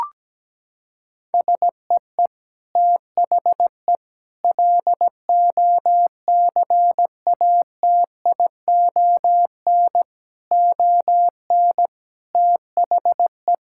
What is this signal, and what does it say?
Morse code, heard as a keyed tone, about 17 wpm, 700 Hz. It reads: SEE THE LOCATION ON THE
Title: Sentences from Top 1000 Words - Review Part 6 - 17wpm